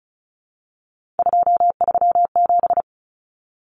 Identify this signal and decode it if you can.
Morse code, heard as a keyed tone, about 34 wpm, 700 Hz. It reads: 237
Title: Numbers - 3-Digit 35wpm